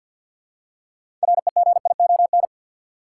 Morse code, heard as a keyed tone, about 53 wpm, 700 Hz. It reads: REGION
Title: Six Letter Words - 50wpm